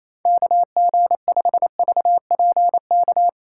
Morse code, heard as a keyed tone, about 28 wpm, 700 Hz. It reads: KG5VPK
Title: USA Call Signs - 2x Character Spacing - 28wpm